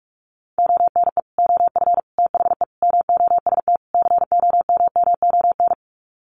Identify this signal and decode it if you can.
Morse code, heard as a keyed tone, about 44 wpm, 700 Hz. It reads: ONE OF THE MOST COMMON
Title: Sets of 5 Words Encore2 45wpm